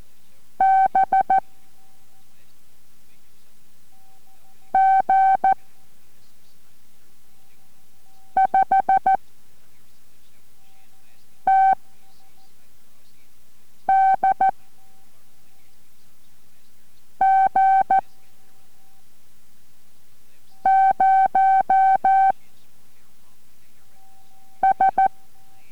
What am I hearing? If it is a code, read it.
Morse code, heard as a keyed tone, about 14 wpm, 770 Hz. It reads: B G 5 T D G 0 S